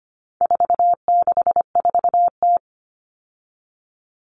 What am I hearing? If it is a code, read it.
Morse code, heard as a keyed tone, about 25 wpm, 700 Hz. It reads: 464T
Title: All Cut Numbers - 4-Digit 25wpm